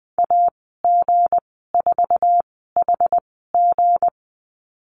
Morse code, heard as a keyed tone, about 20 wpm, spaced as for 14 wpm, 700 Hz. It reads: AG4HG
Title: 179 Lesson 29g - Slash - Call Signs 20wpm